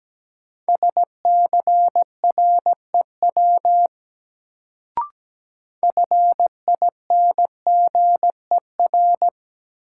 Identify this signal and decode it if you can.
Morse code, heard as a keyed tone, about 17 wpm, 700 Hz. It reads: SCREW FINGER
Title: Sentences from Top 1600 Words - New Word Drill - 17wpm